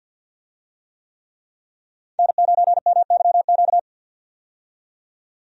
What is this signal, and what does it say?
Morse code, heard as a keyed tone, about 53 wpm, 700 Hz. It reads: N9MYQ
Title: USA Call Signs - 5 Characters - 50wpm